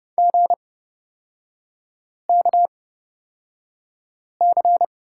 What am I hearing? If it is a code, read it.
Morse code, heard as a keyed tone, about 30 wpm, spaced as for 5 wpm, 700 Hz. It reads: GKC
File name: Lesson 38e - Letter q - Three Characters 30wpm